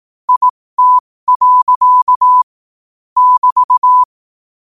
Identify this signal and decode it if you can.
Morse code, heard as a keyed tone, about 18 wpm, spaced as for 15 wpm, 1000 Hz. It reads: IT. <BT>
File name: qotdfiles / 15wpm